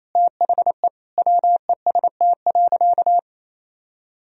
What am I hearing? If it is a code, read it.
Morse code, heard as a keyed tone, about 28 wpm, 700 Hz. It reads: THE WEST.
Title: Sentences from Top 1000 Words - Review Part 2 - 28wpm